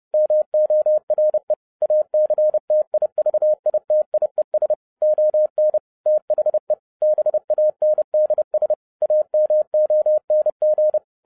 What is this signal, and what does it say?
Morse code, heard as a keyed tone, about 30 wpm, 610 Hz. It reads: MORE ACTIVITIES ON THE BANDS AMONG
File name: 30wpm